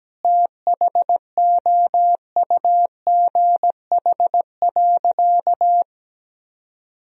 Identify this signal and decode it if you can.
Morse code, heard as a keyed tone, about 17 wpm, 700 Hz. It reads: THOUGH.
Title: Sentences from Top 1000 Words - Review Part 2 - 17wpm